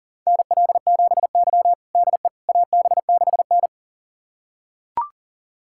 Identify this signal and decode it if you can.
Morse code, heard as a keyed tone, about 40 wpm, 700 Hz. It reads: NL7Y DE AB6N
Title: USA Callsign DE Callsign - 4 Characters - 40wpm